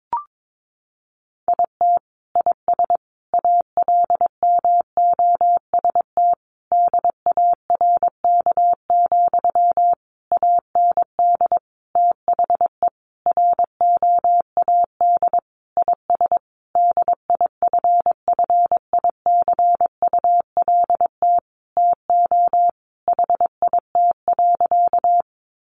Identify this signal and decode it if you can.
Morse code, heard as a keyed tone, about 22 wpm, 700 Hz. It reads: IT IS ALMOST DARK, AND THE ROAD IS DIFFICULT TO HIT.
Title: Sentences from Top 1200 Words 22wpm